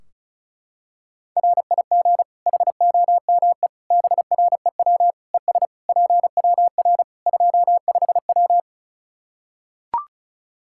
Morse code, heard as a keyed tone, about 36 wpm, 700 Hz. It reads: RIG HOME BREW ES PWR 25W
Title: Practice QSO - Exchange 2 - Info - 35wpm